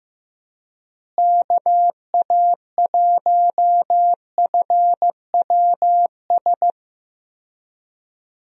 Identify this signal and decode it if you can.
Morse code, heard as a keyed tone, about 15 wpm, 700 Hz. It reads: KA1FWS
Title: USA Call Signs - 6 Characters - Repeated 2x - 15wpm